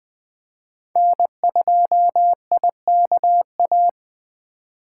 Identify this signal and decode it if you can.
Morse code, heard as a keyed tone, about 20 wpm, 700 Hz. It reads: N2IKA
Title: USA Call Signs - 5 Characters - 3x Character Spacing - 20wpm